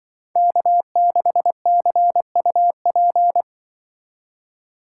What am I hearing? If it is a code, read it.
Morse code, heard as a keyed tone, about 23 wpm, 700 Hz. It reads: K6CUP
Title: Speed Racing - USA 1x3 Call Signs 20wpm